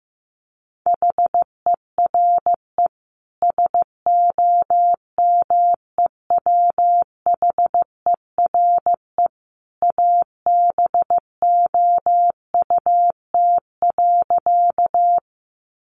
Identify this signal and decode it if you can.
Morse code, heard as a keyed tone, about 15 wpm, 700 Hz. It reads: HERE SOMEWHERE ABOUT.
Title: Sentences from Top 1400 Words 15wpm